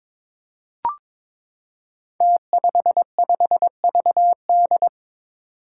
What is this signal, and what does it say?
Morse code, heard as a keyed tone, about 22 wpm, 700 Hz. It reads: T55VD
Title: Five Challenging Letter-Number Combinations 22wpm